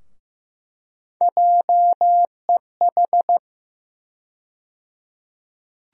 Morse code, heard as a keyed tone, about 15 wpm, 700 Hz. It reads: JEH
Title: USA Call Sign Postfixes - 3 Letters 15wpm